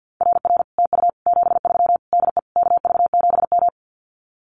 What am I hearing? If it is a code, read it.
Morse code, heard as a keyed tone, about 50 wpm, 700 Hz. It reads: RR TU 73 DE KV7M
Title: Common QSO Phrases Encore 50wpm